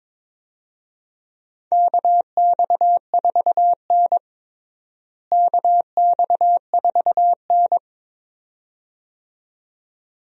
Morse code, heard as a keyed tone, about 22 wpm, 700 Hz. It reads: KX4N KX4N
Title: USA Call Signs - 4 Characters - Repeated 2x - 22wpm